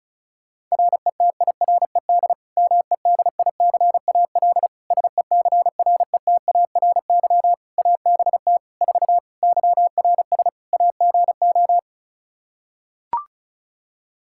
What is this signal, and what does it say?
Morse code, heard as a keyed tone, about 34 wpm, 700 Hz. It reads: RETIRED MEDICAL SECRETARY ABT 4 YRS AGO
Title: Practice QSO - Exchange 3 - More Info - 35wpm